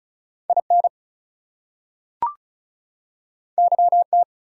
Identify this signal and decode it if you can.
Morse code, heard as a keyed tone, about 36 wpm, 700 Hz. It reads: IN YT
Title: US States and Canadian Provinces 35wpm